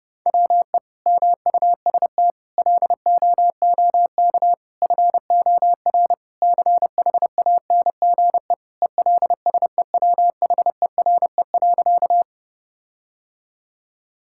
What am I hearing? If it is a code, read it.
Morse code, heard as a keyed tone, about 30 wpm, 700 Hz. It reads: WE MUST LOOK FOR CHANGE ELSEWHERE.
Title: Sentences from Top 1600 Words 30wpm